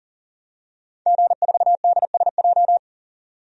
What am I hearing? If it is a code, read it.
Morse code, heard as a keyed tone, about 37 wpm, 700 Hz. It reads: G4DSJ
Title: DXCC-100 Call Signs - 5 Characters - 40wpm